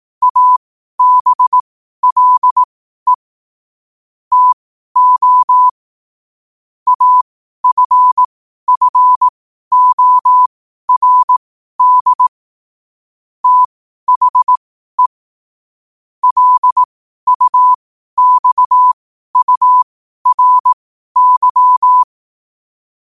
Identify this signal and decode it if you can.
Morse code, heard as a keyed tone, about 18 wpm, spaced as for 12 wpm, 1000 Hz. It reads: ABLE TO AFFORD THE LUXURY
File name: qotdfiles / 12wpm